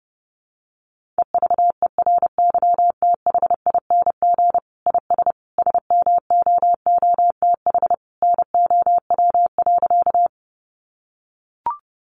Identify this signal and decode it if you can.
Morse code, heard as a keyed tone, about 30 wpm, 700 Hz. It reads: EVERYTHING IS SMOOTH NOW.
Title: Sentences from Top 1400 Words 30wpm